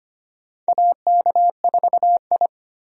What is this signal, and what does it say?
Morse code, heard as a keyed tone, about 25 wpm, 700 Hz. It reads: AK4I
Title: USA Call Signs - 4 Characters - 25wpm